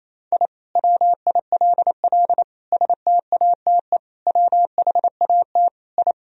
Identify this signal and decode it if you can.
Morse code, heard as a keyed tone, about 28 wpm, 700 Hz. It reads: I WILL STATE WHAT I
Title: Sentences from Top 1000 Words - Review Part 2 - 28wpm